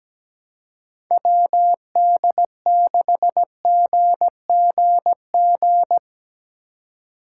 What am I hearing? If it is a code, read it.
Morse code, heard as a keyed tone, about 17 wpm, 700 Hz. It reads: WD6GGG
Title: USA Call Signs - 4x Character Spacing - 17wpm